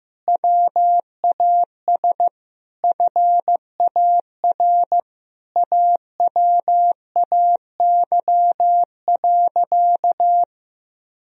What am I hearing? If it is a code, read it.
Morse code, heard as a keyed tone, about 15 wpm, 700 Hz. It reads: WAS FAR AWAY.
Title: Sentences from Top 1000 Words - Review Part 1 - 15wpm